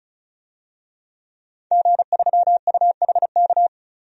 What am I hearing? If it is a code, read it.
Morse code, heard as a keyed tone, about 34 wpm, 700 Hz. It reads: G3UHK
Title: DXCC-100 Call Signs - 5 Characters - 35wpm